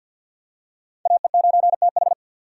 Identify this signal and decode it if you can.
Morse code, heard as a keyed tone, about 53 wpm, 700 Hz. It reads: AE9TH
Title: USA Call Signs - 5 Characters - 4x Character Spacing - 50wpm